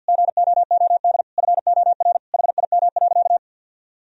Morse code, heard as a keyed tone, about 53 wpm, 700 Hz. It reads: GOOD FOR HIM.
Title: Sentences from Top 1000 Words - Review Part 1 - 50wpm